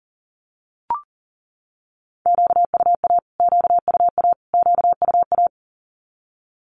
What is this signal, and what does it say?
Morse code, heard as a keyed tone, about 40 wpm, 700 Hz. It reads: QUA QUA QUA
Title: Q-Codes Repeated 3x 40wpm